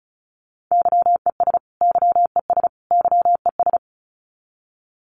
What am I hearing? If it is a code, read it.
Morse code, heard as a keyed tone, about 34 wpm, 700 Hz. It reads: YES YES YES
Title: Top 500 Words Repeated 35wpm